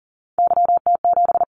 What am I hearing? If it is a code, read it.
Morse code, heard as a keyed tone, about 42 wpm, 700 Hz. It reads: YT7
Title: Callsign Prefixes for Top 25 DXCC - 40wpm